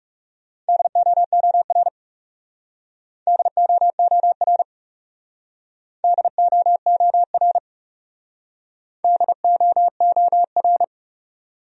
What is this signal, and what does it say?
Morse code, heard as a keyed tone, about 40 wpm, 700 Hz. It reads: DOOR DOOR DOOR DOOR